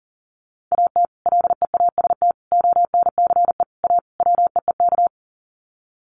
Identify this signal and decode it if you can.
Morse code, heard as a keyed tone, about 37 wpm, 700 Hz. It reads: AT LEAST ONCE A WEEK
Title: Sets of 5 Words Encore2 40wpm